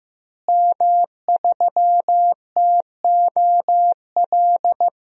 Morse code, heard as a keyed tone, about 15 wpm, 700 Hz. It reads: M3TOL
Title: DXCC-100 Call Signs - 5 Characters - Repeated 3x - 15wpm